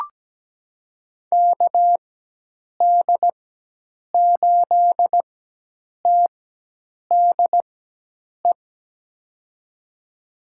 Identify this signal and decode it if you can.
Morse code, heard as a keyed tone, about 17 wpm, spaced as for 8 wpm, 700 Hz. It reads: KD8TDE
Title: USA Call Signs - 4x Character Spacing - 17wpm